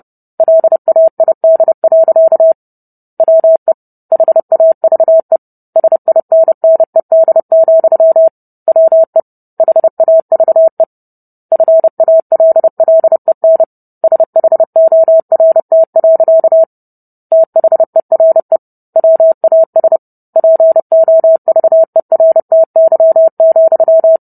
Morse code, heard as a keyed tone, about 30 wpm, 650 Hz. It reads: LAID. WE HAVE SINNED, WE HAVE FALLEN SHORT. THERE WAS POVERTY,